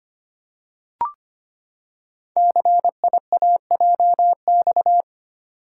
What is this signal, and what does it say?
Morse code, heard as a keyed tone, about 25 wpm, 700 Hz. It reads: CIAJX
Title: Code Groups - 5 Letters - 25wpm